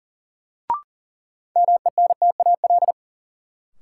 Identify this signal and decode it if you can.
Morse code, heard as a keyed tone, about 40 wpm, 700 Hz. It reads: MENTAL
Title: Sentences from Top 1000 Words - Review Part 4 - New Word Drill - 40wpm